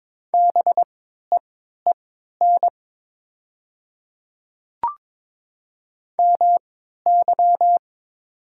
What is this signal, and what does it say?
Morse code, heard as a keyed tone, about 22 wpm, spaced as for 12 wpm, 700 Hz. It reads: BEEN MY
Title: 3x Character Spacing - Top 300 Words - 22wpm